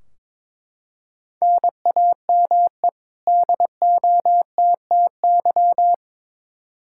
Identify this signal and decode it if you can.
Morse code, heard as a keyed tone, about 22 wpm, 700 Hz. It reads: NAME DOTTY
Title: K1USN SST QSO Practice 22wpm